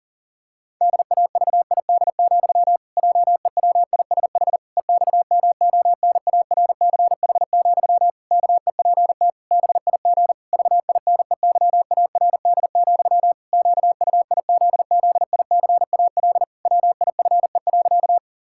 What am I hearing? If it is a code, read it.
Morse code, heard as a keyed tone, about 37 wpm, 700 Hz. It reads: DAVID, JEWISH EXMONARCH, KEPT BIG VINEYARD, QUIZZICAL WIFE.
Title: Pangram Challenge v2 - 40wpm